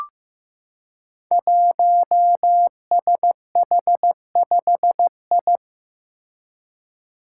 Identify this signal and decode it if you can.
Morse code, heard as a keyed tone, about 15 wpm, 700 Hz. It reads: 1SH5I
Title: Five Challenging Letter-Number Combinations 15wpm